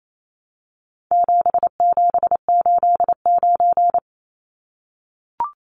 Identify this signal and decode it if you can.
Morse code, heard as a keyed tone, about 28 wpm, 700 Hz. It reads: 7789
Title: Numbers - 4-Digit 28wpm